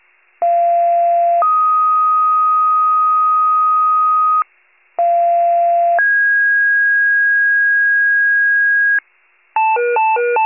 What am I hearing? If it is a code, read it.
Two-tone page, heard as a one-second tone, then a three-second tone, about 690 then 1250 Hz, about 690 then 1650 Hz, with noise. It is heard from 0.4 to 9.0 s.